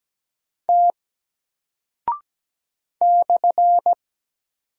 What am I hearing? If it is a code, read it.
Morse code, heard as a keyed tone, about 17 wpm, 700 Hz. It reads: T /
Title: Single Character (Letter:Number:Punctuation:Prosign) Alt 17wpm